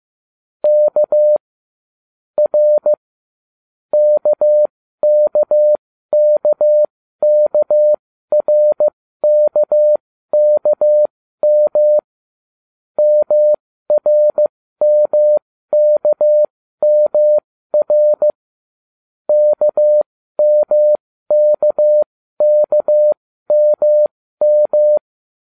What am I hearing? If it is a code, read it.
Morse code, heard as a keyed tone, about 15 wpm, spaced as for 12 wpm, 600 Hz. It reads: K R KKKKRKKM MRMKMR KMKKMM